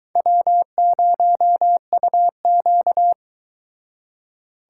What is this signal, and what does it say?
Morse code, heard as a keyed tone, about 23 wpm, 700 Hz. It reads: W0UQ